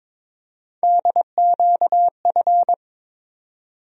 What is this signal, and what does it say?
Morse code, heard as a keyed tone, about 21 wpm, 700 Hz. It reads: DQF
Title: USA Call Sign Postfixes - 3 Letters 22wpm